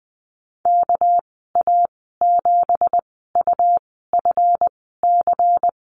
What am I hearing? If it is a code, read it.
Morse code, heard as a keyed tone, about 20 wpm, spaced as for 14 wpm, 700 Hz. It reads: KA7UFC
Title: 231 Lesson 37g - Letter x - Call Signs 20wpm